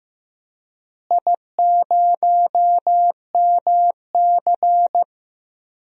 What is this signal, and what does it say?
Morse code, heard as a keyed tone, about 15 wpm, 700 Hz. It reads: I0MC
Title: DXCC-100 Call Signs - 4 Characters - Repeated 3x - 15wpm